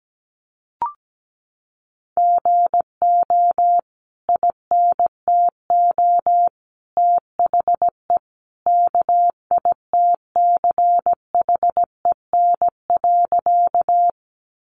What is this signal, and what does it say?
Morse code, heard as a keyed tone, about 17 wpm, 700 Hz. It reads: GO INTO THE KITCHEN.
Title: Sentences from Top 1400 Words 17wpm